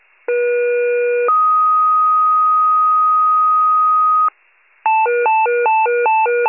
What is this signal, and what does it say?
Two-tone page, heard as a one-second tone, then a three-second tone, about 480 then 1250 Hz. It is heard from 0.3 to 4.3 s.
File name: pager / 23 FIRE / wcalert